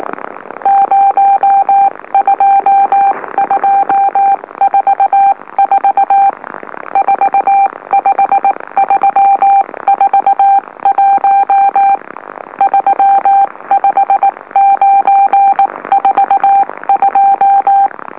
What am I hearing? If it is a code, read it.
Morse code, heard as a keyed tone, about 19 wpm, spaced as for 16 wpm, 790 Hz. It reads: 02244 45341 35942
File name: M04- -2345utc-LOLO